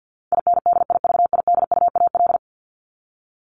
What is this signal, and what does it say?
Morse code, heard as a keyed tone, about 50 wpm, 700 Hz. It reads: INDIVIDUAL
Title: ES:S Suffix - Top 100 Words with Base Words 50wpm